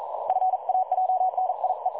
Morse code, heard as a keyed tone, about 38 wpm, 750 Hz. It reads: FAJNIE
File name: dlazaawansowanych / nagraniazpasma / websdr recording 3565.0kHz